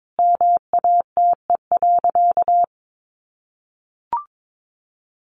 Morse code, heard as a keyed tone, about 22 wpm, 700 Hz. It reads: MATE.
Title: A Sound Of Thunder 22wpm